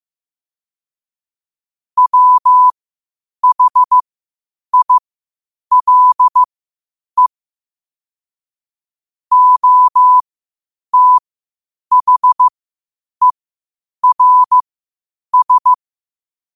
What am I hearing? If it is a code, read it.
Morse code, heard as a keyed tone, about 15 wpm, spaced as for 8 wpm, 1000 Hz. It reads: WHILE OTHERS
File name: qotdfiles / 8wpm